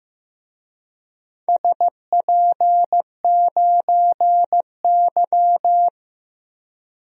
Morse code, heard as a keyed tone, about 15 wpm, 700 Hz. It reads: SP9Y
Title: DXCC-100 Call Signs - 4 Characters - Repeated 2x - 15wpm